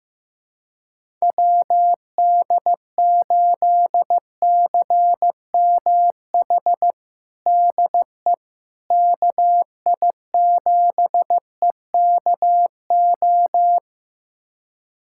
Morse code, heard as a keyed tone, about 15 wpm, 700 Hz. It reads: WD8CMH DE KI7EKO
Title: USA Callsign DE Callsign 15wpm